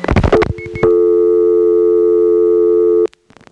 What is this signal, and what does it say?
Dial tone (North America), 350 + 440 Hz.